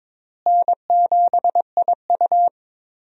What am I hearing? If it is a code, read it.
Morse code, heard as a keyed tone, about 22 wpm, 700 Hz. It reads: N7IU
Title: USA Call Signs - 4 Characters - Repeated 3x - 22wpm